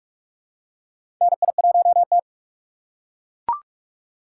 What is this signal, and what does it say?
Morse code, heard as a keyed tone, about 43 wpm, 700 Hz. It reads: NI1T
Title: USA Call Signs - 4 Characters - 6x Character Spacing - 45wpm